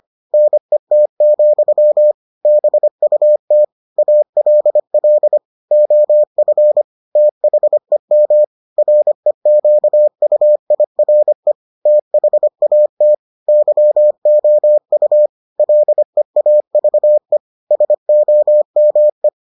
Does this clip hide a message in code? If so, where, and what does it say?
Yes: Morse code from 0.3 to 19.3 s, heard as a keyed tone, about 25 wpm, 600 Hz. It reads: NET, BUT ALL OF THEM REQUIRE THAT YOU LEAVE SOME